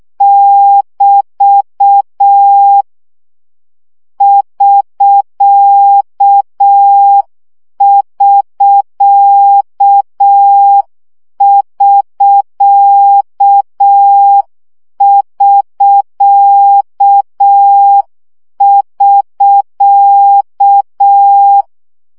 Morse code, heard as a keyed tone, about 6 wpm, 800 Hz. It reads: <BT> <SK><SK><SK><SK><SK>
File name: CW gyakorlas sk 5 baud